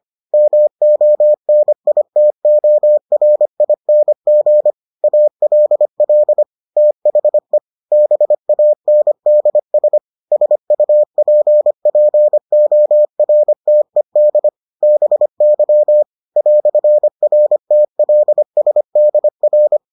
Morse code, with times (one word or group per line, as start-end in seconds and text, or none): 0.3-4.7 MONITORING
5.0-6.4 ALL
6.8-7.6 THE
7.9-10.0 BANDS
10.3-14.5 SUPPORTED
14.8-16.0 BY
16.4-19.8 "RTLSDR